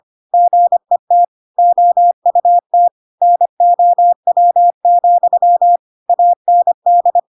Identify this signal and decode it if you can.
Morse code, heard as a keyed tone, about 25 wpm, 700 Hz. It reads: GET OUT NOW, AND
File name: Pinocchio / Pinocchio ch4 25 25 ik1hge